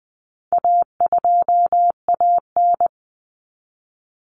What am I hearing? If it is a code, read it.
Morse code, heard as a keyed tone, about 20 wpm, 700 Hz. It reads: A2AN
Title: Common Cut Numbers - 4-Digit 20wpm